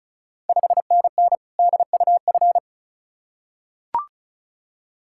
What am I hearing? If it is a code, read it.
Morse code, heard as a keyed tone, about 36 wpm, 690 Hz. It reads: 5NN DUF
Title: Ontario QSO Party Exchange 35wpm